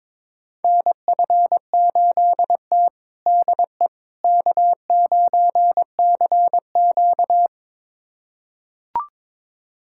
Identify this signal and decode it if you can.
Morse code, heard as a keyed tone, about 21 wpm, 700 Hz. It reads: NF8T DE K9CQ
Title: USA Callsign DE Callsign - 4 Characters - 22wpm